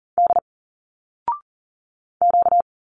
Morse code, heard as a keyed tone, about 40 wpm, 700 Hz. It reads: D Q
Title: Single Letters 40wpm